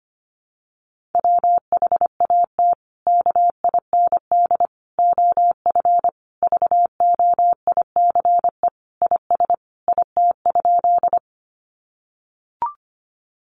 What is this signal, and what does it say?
Morse code, heard as a keyed tone, about 25 wpm, 700 Hz. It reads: WHAT KIND OF VOICE IS IT?
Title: Sentences from Top 500 Words 25wpm